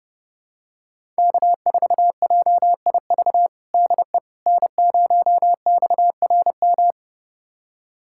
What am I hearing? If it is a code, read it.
Morse code, heard as a keyed tone, about 31 wpm, 700 Hz. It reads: K4JIV DE N0XRM
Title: USA Callsign DE Callsign - 5 Characters - 30wpm